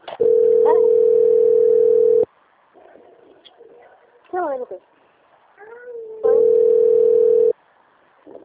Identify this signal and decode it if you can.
Ringback tone (North America), 440 + 480 Hz, with speech.
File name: out- - - - - .22309